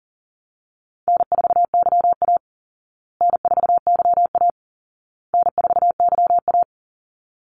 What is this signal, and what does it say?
Morse code, heard as a keyed tone, about 40 wpm, 700 Hz. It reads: N4YA N4YA N4YA
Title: Call Signs Repeated 2x - 40wpm